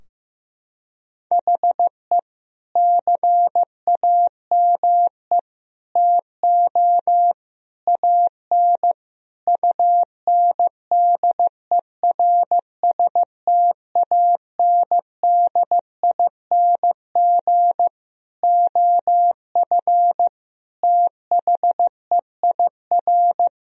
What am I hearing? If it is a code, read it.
Morse code, heard as a keyed tone, about 15 wpm, 700 Hz. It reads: HE CAME TO AN UNDERSTANDING OF THEIR
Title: Sentences from Top 1000 Words - Review Part 3 - 15wpm